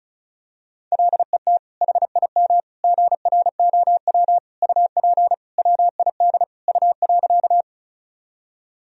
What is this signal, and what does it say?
Morse code, heard as a keyed tone, about 36 wpm, 700 Hz. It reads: LET HIM GROW UP WID U.
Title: Sentences from Top 300 Words and Abbreviations 35wpm